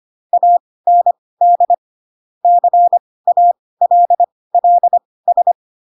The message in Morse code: AND CALLS

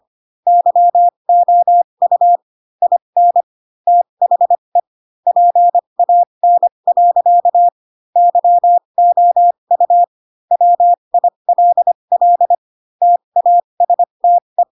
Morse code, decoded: YOU IN THE PAN. YOU WILL TASTE

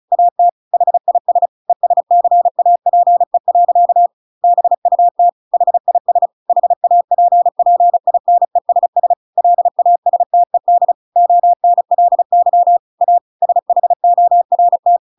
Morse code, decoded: AT HIS ESCAPE. BUT HIS HAPPINESS LASTED ONLY A SHORT

